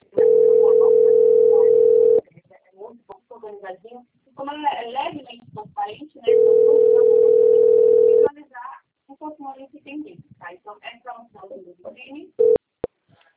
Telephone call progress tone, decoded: ringback tone

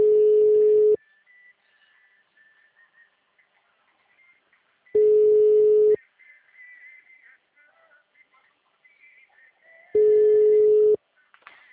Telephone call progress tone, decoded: ringback tone

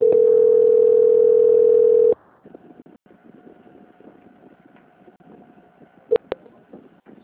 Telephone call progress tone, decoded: ringback tone

